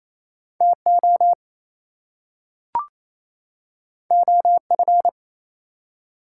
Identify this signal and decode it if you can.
Morse code, heard as a keyed tone, about 27 wpm, 700 Hz. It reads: TO OF
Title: Two Letter Words - 28wpm